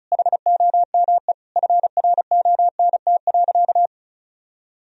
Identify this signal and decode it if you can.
Morse code, heard as a keyed tone, about 34 wpm, 700 Hz. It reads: HOME FRONT.
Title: Sentences from Top 1000 Words - Review Part 2 - 35wpm